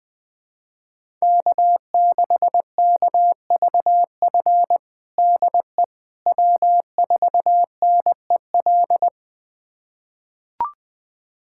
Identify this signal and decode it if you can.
Morse code, heard as a keyed tone, about 20 wpm, 700 Hz. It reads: K6KVF DE W4NEL
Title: USA Callsign DE Callsign - 5 Characters - 20wpm